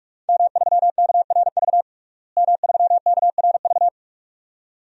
Morse code, heard as a keyed tone, about 46 wpm, 700 Hz. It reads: M3KRV M3KRV
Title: DXCC-100 Call Signs - 5 Characters - Repeated 2x - 45wpm